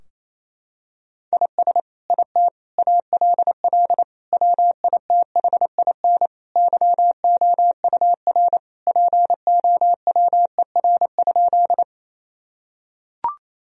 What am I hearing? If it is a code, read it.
Morse code, heard as a keyed tone, about 28 wpm, 700 Hz. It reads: IS IT ALL WITHIN YOUR POWER?
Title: Sentences from Top 1000 Words - Review Part 2 - 28wpm